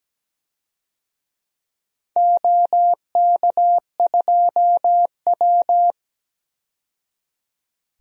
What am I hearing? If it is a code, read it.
Morse code, heard as a keyed tone, about 17 wpm, 700 Hz. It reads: OK2W
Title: DXCC-100 Call Signs - 4 Characters - 17wpm